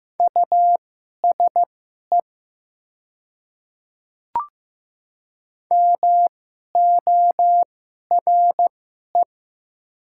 Morse code, heard as a keyed tone, about 15 wpm, spaced as for 11 wpm, 700 Hz. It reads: USE MORE
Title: 2x Character Spacing - Top 500 Words - 15wpm